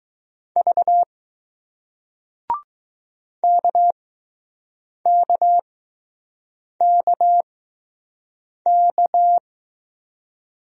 Morse code, heard as a keyed tone, about 20 wpm, spaced as for 6 wpm, 700 Hz. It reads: V KKKK